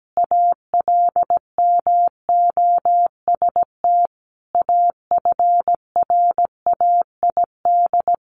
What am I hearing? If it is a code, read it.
Morse code, heard as a keyed tone, about 17 wpm, 700 Hz. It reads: ALMOST AFRAID
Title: Sentences from Top 1500 Words 17wpm